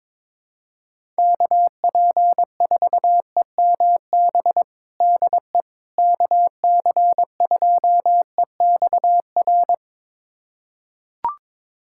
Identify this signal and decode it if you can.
Morse code, heard as a keyed tone, about 22 wpm, 700 Hz. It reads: KP4EMB DE KC2EXR
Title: USA Callsign DE Callsign 22wpm